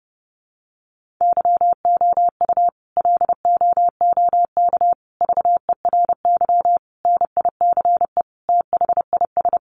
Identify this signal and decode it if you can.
Morse code, heard as a keyed tone, about 30 wpm, 700 Hz. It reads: YOU LOOK VERY NICE THIS
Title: Sentences from Top 1500 Words 30wpm